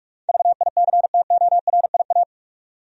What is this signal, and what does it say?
Morse code, heard as a keyed tone, about 46 wpm, 700 Hz. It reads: VICTORIA
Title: USA Common Names - 45wpm